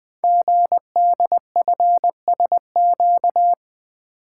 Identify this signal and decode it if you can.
Morse code, heard as a keyed tone, about 20 wpm, 700 Hz. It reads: GDFSQ
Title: Code Groups - 5 Letters - 20wpm